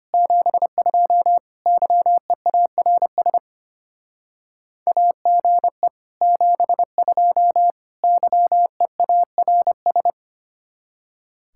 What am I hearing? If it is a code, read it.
Morse code, heard as a keyed tone, about 32 wpm, 700 Hz. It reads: 72 YEARS AGE 72 YEARS